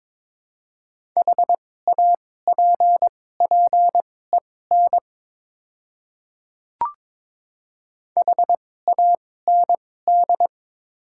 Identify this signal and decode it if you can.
Morse code, heard as a keyed tone, about 22 wpm, spaced as for 16 wpm, 700 Hz. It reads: HAPPEN HAND
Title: 2x Character Spacing - Top 500 Words - 22wpm